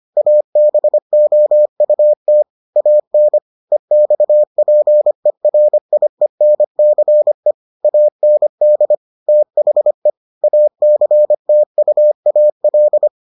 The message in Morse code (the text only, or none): ABOUT AN EXPERIENCE AND THE ACTUAL